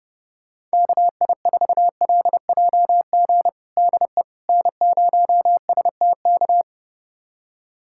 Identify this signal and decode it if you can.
Morse code, heard as a keyed tone, about 32 wpm, 700 Hz. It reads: KI4LJG DE N0STK
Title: USA Callsign DE Callsign 30wpm